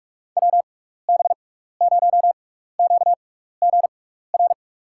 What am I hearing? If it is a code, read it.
Morse code, heard as a keyed tone, about 46 wpm, spaced as for 15 wpm, 700 Hz. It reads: WB0QGR
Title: USA Call Signs - 6x Character Spacing - 45wpm